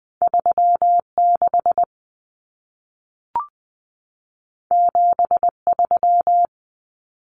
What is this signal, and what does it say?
Morse code, heard as a keyed tone, about 20 wpm, 700 Hz. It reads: 36 73
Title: Common Cut Numbers - 2-Digit 20wpm